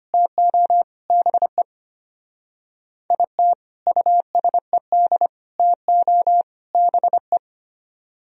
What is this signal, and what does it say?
Morse code, heard as a keyed tone, about 31 wpm, 700 Hz. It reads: TO BE IT USED TO BE
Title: Speed-Racing - Sets of 4 Words 25wpm